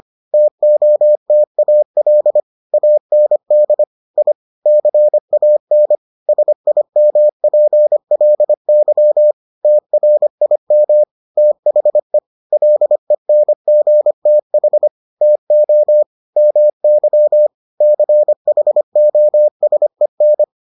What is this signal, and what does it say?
Morse code, heard as a keyed tone, about 25 wpm, 600 Hz. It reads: TOTAL AND I CAN SIMPLY TRIM THE LENGTH TO MY CHOSEN